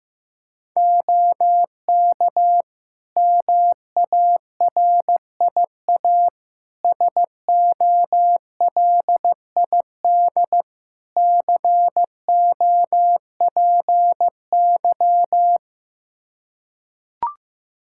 Morse code, heard as a keyed tone, about 15 wpm, 700 Hz. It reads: OK MARIA SOLID COPY
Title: Practice QSO - Exchange 3 - More Info - 15wpm